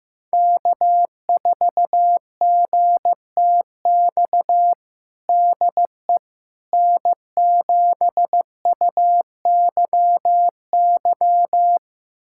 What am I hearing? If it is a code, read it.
Morse code, heard as a keyed tone, about 16 wpm, 700 Hz. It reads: K4GTX DE N7UYY
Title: USA Callsign DE Callsign 15wpm